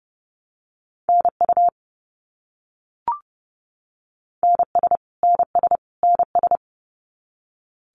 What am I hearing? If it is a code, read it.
Morse code, heard as a keyed tone, about 31 wpm, 700 Hz. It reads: NU NS NS NS
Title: Canadian Provinces and Territories - Repeated 3x - 30wpm